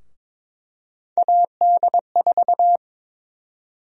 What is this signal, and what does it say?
Morse code, heard as a keyed tone, about 21 wpm, 700 Hz. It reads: AD4
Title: USA Call Sign Prefixes - Repeated 2x - 22wpm